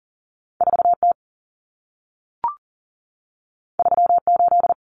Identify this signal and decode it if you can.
Morse code, heard as a keyed tone, about 40 wpm, 700 Hz. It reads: 4T 38
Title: Common Cut Numbers - 2-Digit 40wpm